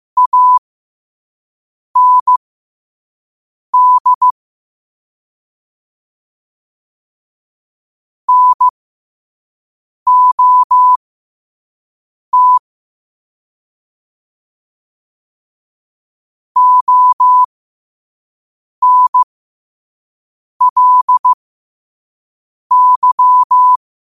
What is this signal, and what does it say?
Morse code, heard as a keyed tone, about 15 wpm, spaced as for 5 wpm, 1000 Hz. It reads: AND NOT ONLY
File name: qotdfiles / 5wpm